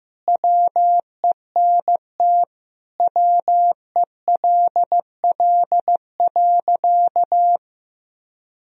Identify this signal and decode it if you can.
Morse code, heard as a keyed tone, about 15 wpm, 700 Hz. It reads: WENT WELL.
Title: Sentences from Top 1000 Words - Review Part 4 - 15wpm